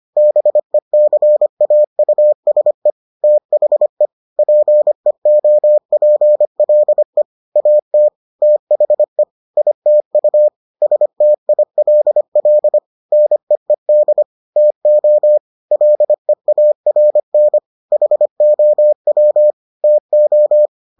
Morse code, heard as a keyed tone, about 25 wpm, 600 Hz. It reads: BECAUSE THE PEOPLE AT THE ITU STILL NEED TO LEARN HOW TO